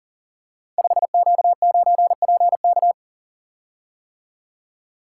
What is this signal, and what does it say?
Morse code, heard as a keyed tone, about 38 wpm, 700 Hz. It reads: 5Q9PK
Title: Code Groups - 5 Letter:Numbers - 40wpm